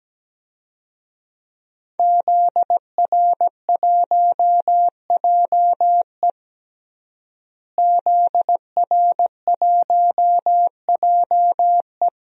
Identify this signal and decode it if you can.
Morse code, heard as a keyed tone, about 17 wpm, 700 Hz. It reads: ZR1JE ZR1JE
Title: DXCC-100 Call Signs - 5 Characters - Repeated 3x - 17wpm